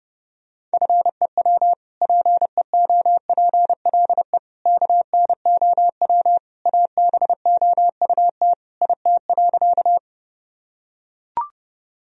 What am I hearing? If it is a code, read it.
Morse code, heard as a keyed tone, about 32 wpm, 700 Hz. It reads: FEW PEOPLE KNOW ABOUT IT.
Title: Sentences from Top 1000 Words - Review Part 1 - 30wpm